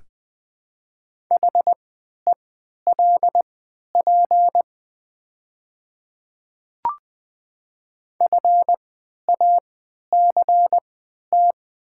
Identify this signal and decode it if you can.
Morse code, heard as a keyed tone, about 20 wpm, spaced as for 11 wpm, 700 Hz. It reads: HELP FACT
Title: 3x Character Spacing - Top 500 Words - 20wpm